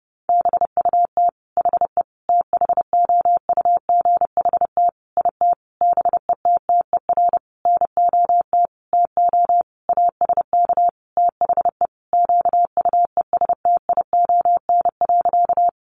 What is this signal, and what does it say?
Morse code, heard as a keyed tone, about 30 wpm, 700 Hz. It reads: BUT HE THOUGHT IT BETTER NOT TO ASK THE QUESTION.
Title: Sentences from Top 600 Words 30wpm